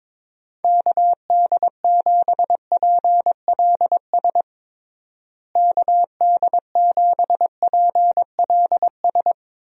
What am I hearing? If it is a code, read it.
Morse code, heard as a keyed tone, about 22 wpm, 700 Hz. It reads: KD7PLS KD7PLS
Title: USA Call Signs - 6 Characters - Repeated 2x - 22wpm